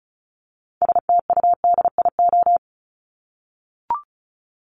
Morse code, heard as a keyed tone, about 34 wpm, 700 Hz. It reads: STUDIO
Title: Top 1001-1250 Words 35wpm